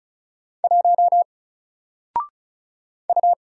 Morse code, heard as a keyed tone, about 36 wpm, 700 Hz. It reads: 1 U
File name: Single-Letter-Number-35wpm